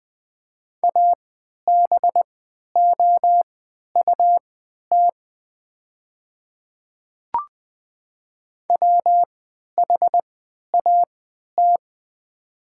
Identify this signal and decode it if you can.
Morse code, heard as a keyed tone, about 20 wpm, spaced as for 11 wpm, 700 Hz. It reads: ABOUT WHAT
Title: 3x Character Spacing - Top 100 Words - 20wpm